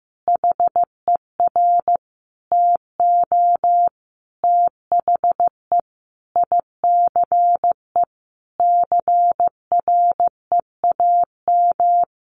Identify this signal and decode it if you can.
Morse code, heard as a keyed tone, about 15 wpm, 700 Hz. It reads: HER TO THE ICE CREAM
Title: Easy Phrases 15wpm